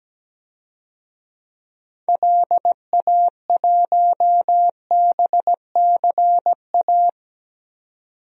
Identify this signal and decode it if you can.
Morse code, heard as a keyed tone, about 17 wpm, 700 Hz. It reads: LA1BCA
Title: DXCC-100 Call Signs - 6 Characters - 17wpm